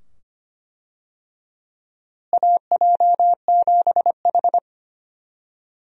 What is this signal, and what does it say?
Morse code, heard as a keyed tone, about 25 wpm, 700 Hz. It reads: AJ7H
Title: USA Call Signs - 4 Characters - Repeated 2x - 25wpm